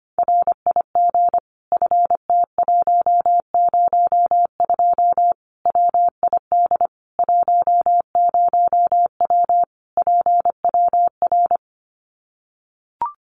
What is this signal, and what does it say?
Morse code, heard as a keyed tone, about 25 wpm, 700 Hz. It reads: RIG FT102 WID 10W PWR
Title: Common QSO Phrases 25wpm